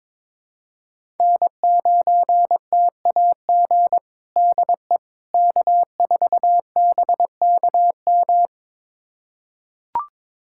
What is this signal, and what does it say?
Morse code, heard as a keyed tone, about 22 wpm, 700 Hz. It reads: N9TAG DE K4BKM
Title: USA Callsign DE Callsign - 5 Characters - 22wpm